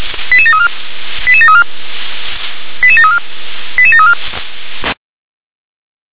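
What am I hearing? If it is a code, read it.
Five-tone selective call (ZVEI), heard as quick runs of five tones, tones of about 70 ms. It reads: 88734 88734 88734 88734
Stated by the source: ZVEI five-tone selective call 88913, received by SDR.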